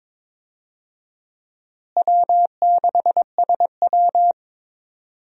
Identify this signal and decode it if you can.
Morse code, heard as a keyed tone, about 21 wpm, 700 Hz. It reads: W6SW
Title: USA Call Signs - 4 Characters - Repeated 2x - 22wpm